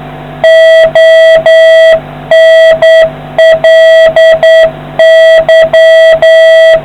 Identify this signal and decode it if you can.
Morse code, heard as a keyed tone, about 10 wpm, 650 Hz, with noise. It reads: ONLY